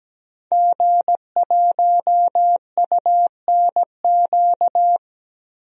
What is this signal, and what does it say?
Morse code, heard as a keyed tone, about 17 wpm, 700 Hz. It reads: G1UNQ
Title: DXCC-100 Call Signs - 5 Characters - Repeated 3x - 17wpm